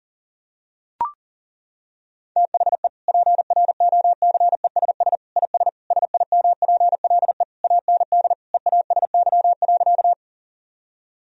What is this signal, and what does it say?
Morse code, heard as a keyed tone, about 40 wpm, 700 Hz. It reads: THE PROCESS IS SIMPLE AND EASY.
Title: Sentences from Top 1000 Words - Review Part 3 - 40wpm